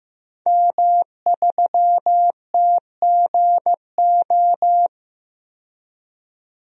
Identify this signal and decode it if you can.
Morse code, heard as a keyed tone, about 15 wpm, 700 Hz. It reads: M3TGO
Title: DXCC-100 Call Signs - 5 Characters - 15wpm